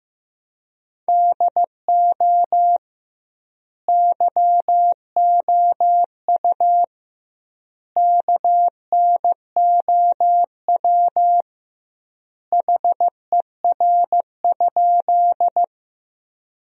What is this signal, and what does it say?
Morse code, heard as a keyed tone, about 16 wpm, 700 Hz. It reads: DO YOU KNOW HER?
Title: 2x Word-Spacing - Sentences from Top 100 Words - 15wpm